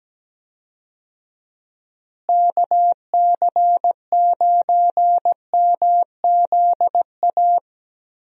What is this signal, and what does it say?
Morse code, heard as a keyed tone, about 17 wpm, 700 Hz. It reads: KC9MZA
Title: USA Call Signs - 6 Characters - Repeated 3x - 17wpm